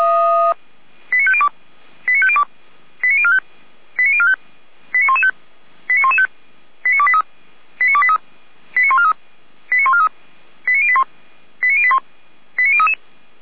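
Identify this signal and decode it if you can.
Five-tone selective call (ZVEI), heard as quick runs of five tones, tones of about 70 ms. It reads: 89502 89502 89045 89045 89115 89115 89283 89283 89143 89143 89081 89081 89033
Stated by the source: ZVEI five-tone selective call, code 89160, received by SDR, Feuerwehr Orlamünde.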